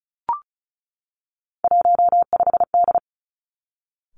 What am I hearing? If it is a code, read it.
Morse code, heard as a keyed tone, about 34 wpm, 700 Hz. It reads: 15D
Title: Three Challenging Letter-Number Combinations 35wpm